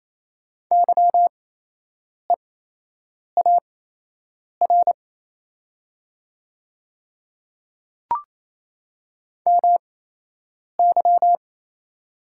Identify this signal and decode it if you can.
Morse code, heard as a keyed tone, about 27 wpm, spaced as for 8 wpm, 700 Hz. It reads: YEAR MY
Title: 8x Character Spacing - Top 200 Words - 28wpm